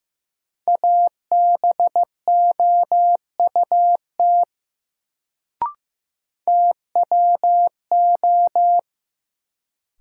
Morse code, heard as a keyed tone, about 15 wpm, 700 Hz. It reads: ABOUT TWO
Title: Sentences from Top 100 Words - New Word Drill - 15wpm